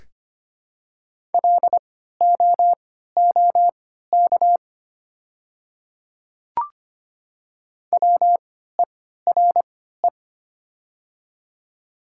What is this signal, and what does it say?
Morse code, heard as a keyed tone, about 25 wpm, spaced as for 14 wpm, 700 Hz. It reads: LOOK WERE
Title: 3x Character Spacing - Top 200 Words - 25wpm